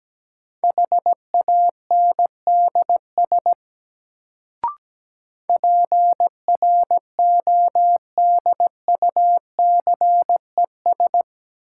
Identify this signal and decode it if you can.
Morse code, heard as a keyed tone, about 17 wpm, 700 Hz. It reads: HANDS PRODUCES
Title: Sentences from Top 1000 Words - Review Part 6 - New Word Drill - 17wpm